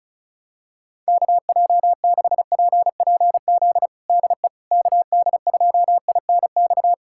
Morse code, heard as a keyed tone, about 36 wpm, 700 Hz. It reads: KJ6PPZ DE KD2INX
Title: USA Callsign DE Callsign 35wpm